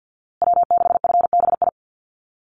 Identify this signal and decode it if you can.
Morse code, heard as a keyed tone, about 50 wpm, 700 Hz. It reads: W6FBI
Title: USA Call Signs Repeated 2x 50wpm